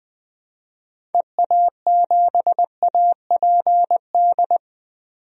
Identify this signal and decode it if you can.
Morse code, heard as a keyed tone, about 20 wpm, 700 Hz. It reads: EA7APD
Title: DXCC-100 Call Signs - 6 Characters - 20wpm